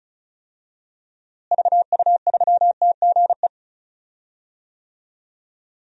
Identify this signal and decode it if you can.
Morse code, heard as a keyed tone, about 36 wpm, 700 Hz. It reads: VU3TGE
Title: DXCC-100 Call Signs - 6 Characters - Repeated 2x - 35wpm